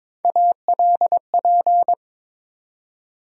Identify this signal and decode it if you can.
Morse code, heard as a keyed tone, about 21 wpm, 700 Hz. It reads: ALP
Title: USA Call Sign Postfixes - 3 Letters 22wpm